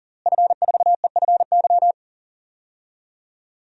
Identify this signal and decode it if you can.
Morse code, heard as a keyed tone, about 40 wpm, 700 Hz. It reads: F4EFY
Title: DXCC-100 Call Signs - 5 Characters - Repeated 3x - 40wpm